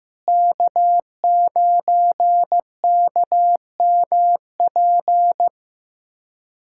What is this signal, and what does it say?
Morse code, heard as a keyed tone, about 15 wpm, 700 Hz. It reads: K9KMP
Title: USA Callsign DE Callsign 15wpm